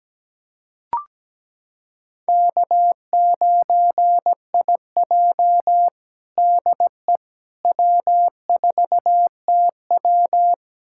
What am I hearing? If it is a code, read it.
Morse code, heard as a keyed tone, about 17 wpm, 700 Hz. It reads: K9IJ DE W4TW
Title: USA Callsign DE Callsign - 4 Characters - 17wpm